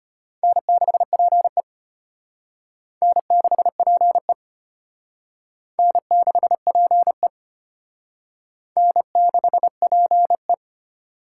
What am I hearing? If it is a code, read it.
Morse code, heard as a keyed tone, about 32 wpm, 700 Hz. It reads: N6PE N6PE N6PE N6PE